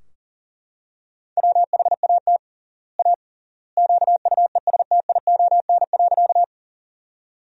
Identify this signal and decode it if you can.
Morse code, heard as a keyed tone, about 40 wpm, 700 Hz. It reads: WHAT A QUESTION.
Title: 3x Word-Spacing - Sentences from Top 400 Words - 40wpm